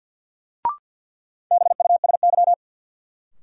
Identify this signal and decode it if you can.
Morse code, heard as a keyed tone, about 53 wpm, 700 Hz. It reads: BUSY
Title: Sentences from Top 1500 Words - New Word Drill Alt - 50wpm